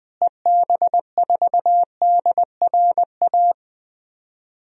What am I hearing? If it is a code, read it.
Morse code, heard as a keyed tone, about 20 wpm, 700 Hz. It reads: EB4DRA
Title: DXCC-100 Call Signs - 6 Characters - Repeated 3x - 20wpm